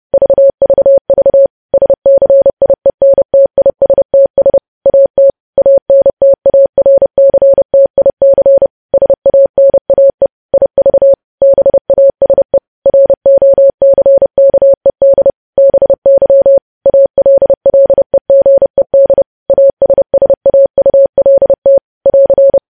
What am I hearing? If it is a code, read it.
Morse code, heard as a keyed tone, about 30 wpm, 550 Hz. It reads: VVV SCIENTISTS AT ANTARCTIC SANAE IV BASE ROCKED BY ALLEGED ASSAULT <AR>
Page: news headlines in morse code at 30 wpm